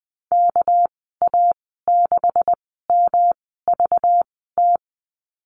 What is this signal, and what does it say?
Morse code, heard as a keyed tone, about 19 wpm, spaced as for 14 wpm, 700 Hz. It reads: KA6MVT
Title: 207 Lesson 33g - Letter j - Call Signs 20wpm